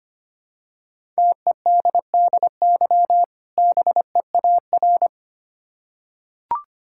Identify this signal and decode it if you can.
Morse code, heard as a keyed tone, about 25 wpm, 700 Hz. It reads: TEDDY BEAR
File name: practice / 25wpm / Difficult Words with Hints - 25wpm